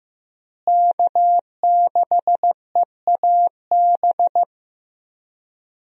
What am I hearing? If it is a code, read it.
Morse code, heard as a keyed tone, about 15 wpm, 700 Hz. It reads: K6EAB
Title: USA Call Signs - 5 Characters - Repeated 2x - 15wpm